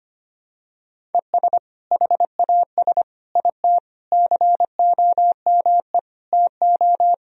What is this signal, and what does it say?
Morse code, heard as a keyed tone, about 25 wpm, 700 Hz. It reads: ES HAS IT COME TO
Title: Sentences from Top 100 Words and Abbreviations 25wpm